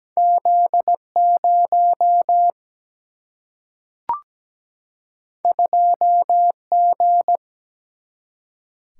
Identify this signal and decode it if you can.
Morse code, heard as a keyed tone, about 17 wpm, 700 Hz. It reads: Z 0 2 G
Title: Single-Letter Single-Number 17wpm